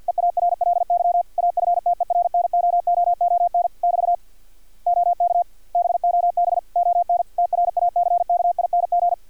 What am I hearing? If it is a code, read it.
Morse code, heard as a keyed tone, about 50 wpm, 700 Hz, with radio noise. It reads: EARLY AFTERNOOON <BT> OK BOB ON TRACKING
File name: archived / CRSnet